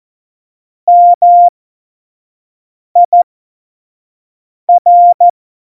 Morse code, heard as a keyed tone, about 14 wpm, spaced as for 5 wpm, 700 Hz. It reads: MIR
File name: Morsekurs / Witze / Witz 22. Januar 2019, 25 F70